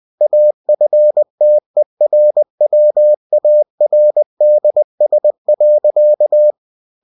Morse code, heard as a keyed tone, about 20 wpm, 600 Hz. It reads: AFTERWARDS.
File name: cw-books / winnie-the-pooh / winnie-20wpm-page-0048